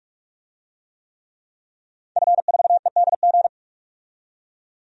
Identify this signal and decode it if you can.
Morse code, heard as a keyed tone, about 46 wpm, 700 Hz. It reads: F4EDG
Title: DXCC-100 Call Signs - 5 Characters - 45wpm